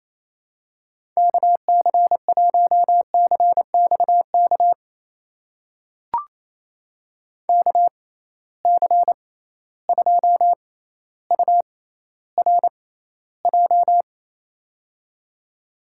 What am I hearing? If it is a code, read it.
Morse code, heard as a keyed tone, about 28 wpm, spaced as for 14 wpm, 700 Hz. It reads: KC1CXK KC2URJ
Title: USA Call Signs - 6 Characters - 6x Character Spacing - 28wpm